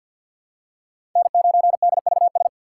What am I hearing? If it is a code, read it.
Morse code, heard as a keyed tone, about 47 wpm, 700 Hz. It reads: N9DVS
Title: USA Call Signs - 5 Characters - 3x Character Spacing - 50wpm